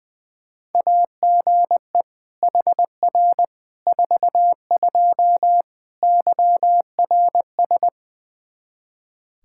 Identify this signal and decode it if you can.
Morse code, heard as a keyed tone, about 20 wpm, 700 Hz. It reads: AGE HR 42 YRS
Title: Practice QSO - Exchange 3 - More Info - 20wpm